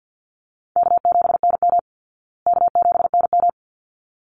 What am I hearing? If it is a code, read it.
Morse code, heard as a keyed tone, about 50 wpm, 700 Hz. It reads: K7NM K7NM
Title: Call Signs Repeated 3x - 50wpm